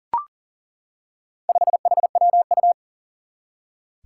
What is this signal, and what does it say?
Morse code, heard as a keyed tone, about 40 wpm, 700 Hz. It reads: 5HWU
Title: Four Challenging Letter-Number Combinations 40wpm